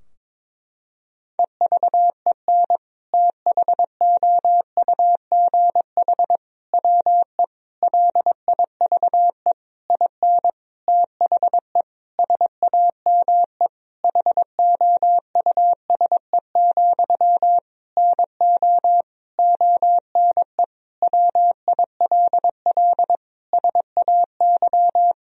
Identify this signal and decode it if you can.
Morse code, heard as a keyed tone, about 22 wpm, 700 Hz. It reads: EVEN THOUGH WE LIVE IN THE SAME HOUSE, NO ONE WILL SAY
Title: Sentences from Top 1000 Words - Review Part 2 - 22wpm